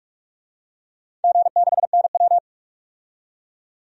Morse code, heard as a keyed tone, about 48 wpm, 700 Hz. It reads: G6NW
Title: DXCC-100 Call Signs - 4 Characters - 45wpm